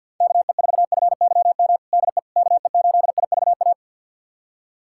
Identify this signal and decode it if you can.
Morse code, heard as a keyed tone, about 50 wpm, 700 Hz. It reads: KE4FYM DE KE8IVA
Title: USA Callsign DE Callsign - 6 Characters - 50wpm